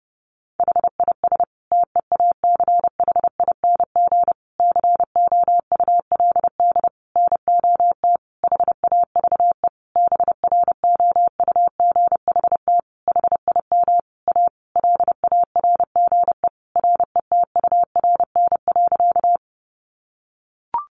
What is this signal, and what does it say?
Morse code, heard as a keyed tone, about 30 wpm, 700 Hz. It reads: HIS TEACHING COULD NOT HAVE BROUGHT HIM A LARGE RETURN.
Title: Sentences from Top 1100 Words 30wpm